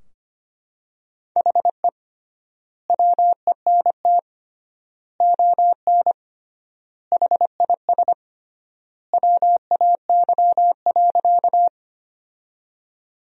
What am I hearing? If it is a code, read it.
Morse code, heard as a keyed tone, about 25 wpm, 700 Hz. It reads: HE WENT ON HIS WAY.
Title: 3x Word-Spacing - Sentences from Top 200 Words - 25wpm